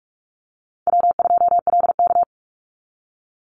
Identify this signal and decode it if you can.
Morse code, heard as a keyed tone, about 44 wpm, 700 Hz. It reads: W2LK
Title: Easy Call Signs Repeated 3x - 45wpm